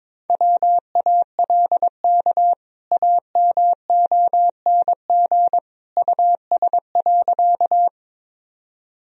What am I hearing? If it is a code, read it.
Morse code, heard as a keyed tone, about 21 wpm, 700 Hz. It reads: WALK AMONG US.
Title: Sentences from Top 1000 Words - Review Part 3 - 22wpm